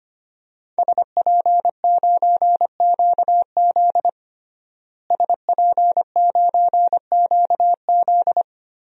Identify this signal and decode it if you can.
Morse code, heard as a keyed tone, about 25 wpm, 700 Hz. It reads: SP9QZ SP9QZ
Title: DXCC-100 Call Signs - 5 Characters - Repeated 3x - 25wpm